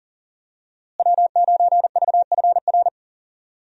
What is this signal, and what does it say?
Morse code, heard as a keyed tone, about 40 wpm, 700 Hz. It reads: W9VFR
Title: USA Call Signs - 5 Characters - Repeated 3x - 40wpm